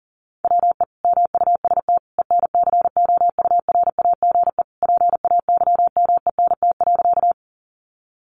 Morse code, heard as a keyed tone, about 40 wpm, 700 Hz. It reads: WE MUST ENCOURAGE PAYMENT.
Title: Sentences from Top 1400 Words 40wpm